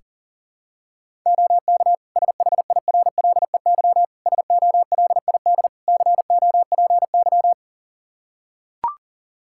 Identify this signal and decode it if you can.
Morse code, heard as a keyed tone, about 40 wpm, 700 Hz. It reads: OK SHIRLEY SOLID COPY
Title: Practice QSO - Exchange 3 - More Info - 40wpm